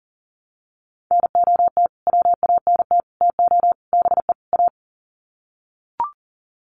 Morse code, heard as a keyed tone, about 38 wpm, 700 Hz. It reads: NOT WANT TO BE A
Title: Sets of 5 Words Encore 40wpm